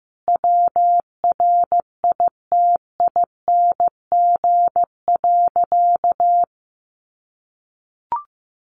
Morse code, heard as a keyed tone, about 14 wpm, 700 Hz. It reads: WRITING.
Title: Sentences from Top 800 Words 15wpm